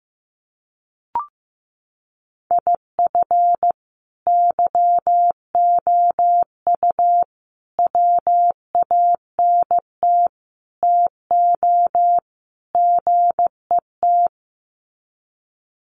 Morse code, heard as a keyed tone, about 15 wpm, 700 Hz. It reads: IF YOU WANT TO GET
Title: Sets of 5 Words Encore 15wpm